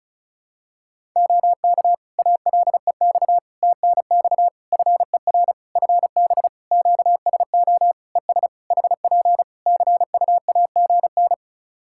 Morse code, heard as a keyed tone, about 36 wpm, 700 Hz. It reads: OK ALEX TNX FER FB QSO ES HP CUAGN
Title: Practice QSO - Exchange 4 - Ending - 35wpm